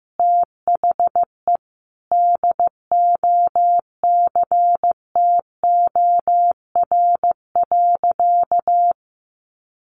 Morse code, heard as a keyed tone, about 14 wpm, 700 Hz. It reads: THE DOCTOR.
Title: Sentences from Top 1500 Words 15wpm